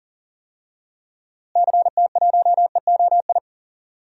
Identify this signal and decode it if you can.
Morse code, heard as a keyed tone, about 40 wpm, 700 Hz. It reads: CT1EOI
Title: DXCC-100 Call Signs - 6 Characters - Repeated 3x - 40wpm